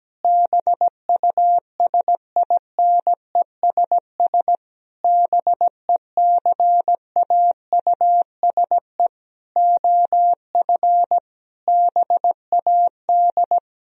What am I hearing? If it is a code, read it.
Morse code, heard as a keyed tone, about 17 wpm, 700 Hz. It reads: BUSINESS BECAUSE OF BAD
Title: Sentences from Top 1000 Words - Review Part 4 - 17wpm